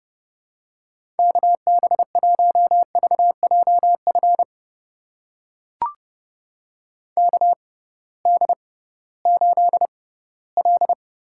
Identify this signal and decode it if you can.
Morse code, heard as a keyed tone, about 32 wpm, spaced as for 18 wpm, 700 Hz. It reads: KB1VJF KD8L
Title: USA Call Signs - 6 Characters - 6x Character Spacing - 30wpm